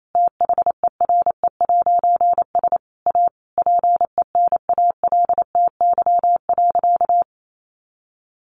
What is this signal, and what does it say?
Morse code, heard as a keyed tone, about 28 wpm, 700 Hz. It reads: THERE'S A PENALTY.
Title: A Sound Of Thunder 28wpm